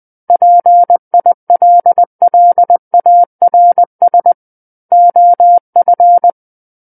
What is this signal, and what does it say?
Morse code, heard as a keyed tone, about 20 wpm, 700 Hz. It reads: PILLARS OF
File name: 11-Chapter 11 - 20 WPM